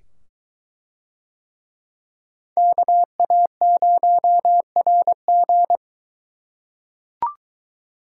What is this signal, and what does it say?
Morse code, heard as a keyed tone, about 22 wpm, 700 Hz. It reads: KA0RG